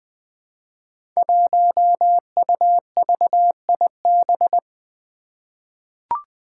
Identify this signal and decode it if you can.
Morse code, heard as a keyed tone, about 20 wpm, 700 Hz. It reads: 1UVIB
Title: Five Challenging Letter-Number Combinations 20wpm